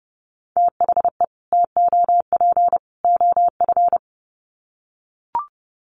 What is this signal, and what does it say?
Morse code, heard as a keyed tone, about 30 wpm, 700 Hz. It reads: THE TOP OF
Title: Sets of 3 Words 30wpm